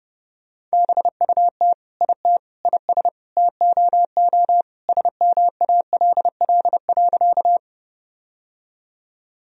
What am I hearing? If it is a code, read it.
Morse code, heard as a keyed tone, about 30 wpm, 700 Hz. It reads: BUT IT IS TOO SMALL.
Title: Sentences from Top 1000 Words - Review Part 1 - 30wpm